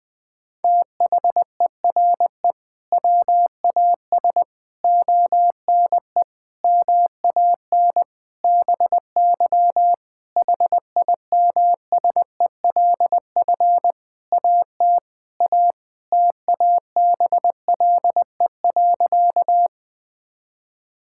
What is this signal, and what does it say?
Morse code, heard as a keyed tone, about 19 wpm, 700 Hz. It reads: THERE WAS ONE MAN BY HIMSELF AT A TABLE.
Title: Sentences from Top 1600 Words 20wpm